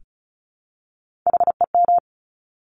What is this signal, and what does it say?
Morse code, heard as a keyed tone, about 36 wpm, 700 Hz. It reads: HEM
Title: Top Three Letter Combinations in Words 35wpm